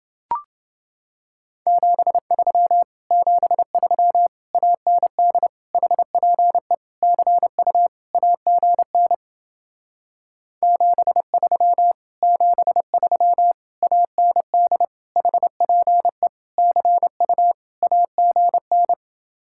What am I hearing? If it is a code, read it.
Morse code, heard as a keyed tone, about 29 wpm, 700 Hz. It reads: 73 73 AND HPE CU AGN 73 73 AND HPE CU AGN